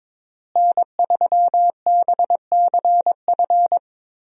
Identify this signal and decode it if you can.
Morse code, heard as a keyed tone, about 21 wpm, 700 Hz. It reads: N3BCF
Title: USA Call Signs - 5 Characters - Repeated 3x - 22wpm